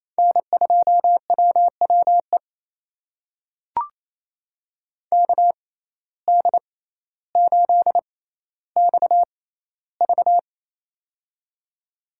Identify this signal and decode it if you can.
Morse code, heard as a keyed tone, about 27 wpm, spaced as for 14 wpm, 700 Hz. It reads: N2WWE KD8XV
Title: USA Call Signs - 5 Characters - 6x Character Spacing - 28wpm